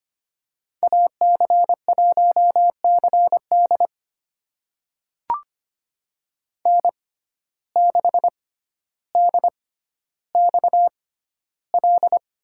Morse code, heard as a keyed tone, about 25 wpm, spaced as for 13 wpm, 700 Hz. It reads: AC1CD N6DXL
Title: USA Call Signs - 6x Character Spacing - 25wpm